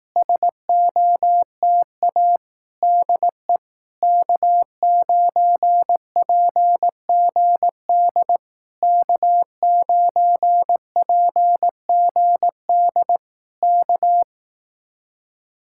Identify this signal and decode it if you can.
Morse code, heard as a keyed tone, about 18 wpm, 700 Hz. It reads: SOTA DE K9PGD K9PGD K
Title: Speed-Racing Common QSO Phrases Encore - 15wpm